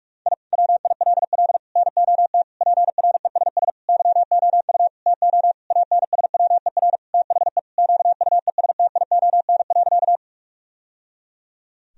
Morse code, heard as a keyed tone, about 41 wpm, 700 Hz. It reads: I WILL NOT PRESS YOU TO ANSWER THE QUESTION.
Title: Sentences from Top 1000 Words - Review Part 1 - 45wpm